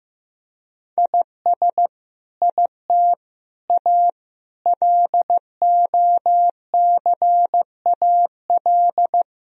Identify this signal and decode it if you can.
Morse code, heard as a keyed tone, about 15 wpm, 700 Hz. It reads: IS IT A LOCAL
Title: Sentences from Top 1000 Words - Review Part 3 - 15wpm